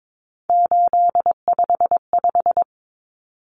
Morse code, heard as a keyed tone, about 22 wpm, 700 Hz. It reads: 855
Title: Common Cut Numbers - 3-Digit 22wpm